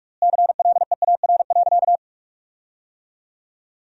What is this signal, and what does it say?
Morse code, heard as a keyed tone, about 44 wpm, 690 Hz. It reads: CLEAR.
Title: 4x Word-Spacing - Sentences from Top 500 Words - 45wpm